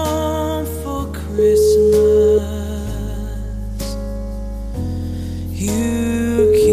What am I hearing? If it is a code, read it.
Ringback tone (Europe, 425 Hz), with noise.